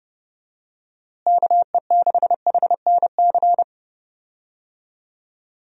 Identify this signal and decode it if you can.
Morse code, heard as a keyed tone, about 30 wpm, 700 Hz. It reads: KE6HNC
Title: USA Call Signs - 6 Characters - Repeated 2x - 30wpm